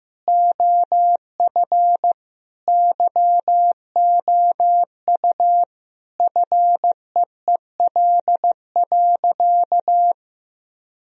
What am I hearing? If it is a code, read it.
Morse code, heard as a keyed tone, about 16 wpm, 700 Hz. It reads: OF YOU FEEL.
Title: Sentences from Top 1000 Words - Review Part 1 - 15wpm